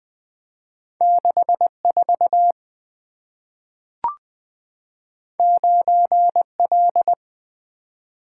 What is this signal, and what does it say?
Morse code, heard as a keyed tone, about 20 wpm, 700 Hz. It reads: 64 9L
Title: Two Letter-Number Combinations - 20wpm